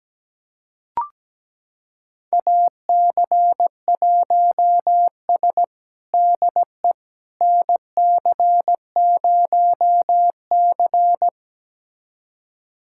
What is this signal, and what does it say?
Morse code, heard as a keyed tone, about 17 wpm, 700 Hz. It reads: AC1S DE NC0C
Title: USA Callsign DE Callsign - 4 Characters - 17wpm